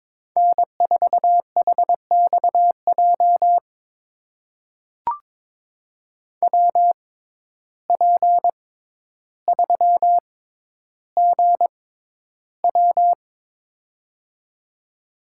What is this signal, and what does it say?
Morse code, heard as a keyed tone, about 22 wpm, spaced as for 11 wpm, 700 Hz. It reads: N4HXJ WP3GW
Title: USA Call Signs - 5 Characters - 6x Character Spacing - 22wpm